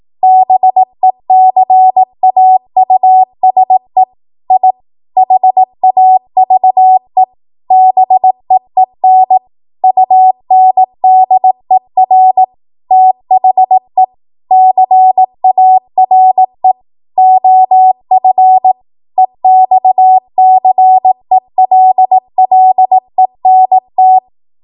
Morse code, heard as a keyed tone, about 18 wpm, 750 Hz. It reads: BECAUSE I HAVE BEEN UNDER THE CARE OF EXCELLENT
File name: Morse / 18WPM